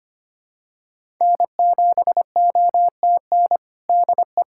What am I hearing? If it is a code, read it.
Morse code, heard as a keyed tone, about 25 wpm, 700 Hz. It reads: N7OTN DE
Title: USA Callsign DE Callsign - 5 Characters - 25wpm